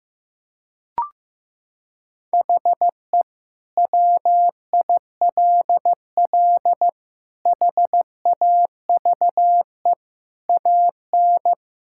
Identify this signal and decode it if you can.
Morse code, heard as a keyed tone, about 15 wpm, 700 Hz. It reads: HE WILL HAVE AN
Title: Sentences from Top 1000 Words - Review Part 5 - 15wpm